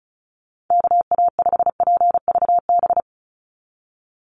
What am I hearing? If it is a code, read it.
Morse code, heard as a keyed tone, about 34 wpm, 700 Hz. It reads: KA5PVB
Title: NA SOTA Chaser Callsigns 35wpm